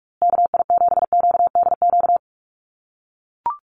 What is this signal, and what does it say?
Morse code, heard as a keyed tone, about 44 wpm, 700 Hz. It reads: KI7QDQ
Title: USA Call Signs Repeated 2x 45wpm